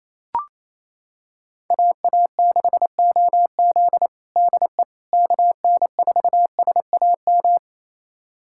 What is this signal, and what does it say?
Morse code, heard as a keyed tone, about 28 wpm, 700 Hz. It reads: AA6OZ DE KN4SAM
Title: USA Callsign DE Callsign 28wpm